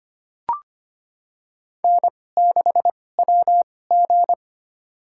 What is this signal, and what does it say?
Morse code, heard as a keyed tone, about 25 wpm, spaced as for 18 wpm, 700 Hz. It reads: N6WG
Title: USA Call Signs - 4 Characters - 2x Character Spacing - 25wpm